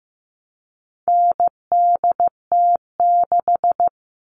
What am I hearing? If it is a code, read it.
Morse code, heard as a keyed tone, about 15 wpm, 700 Hz. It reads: NDT6
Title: All Cut Numbers - 4-Digit 15wpm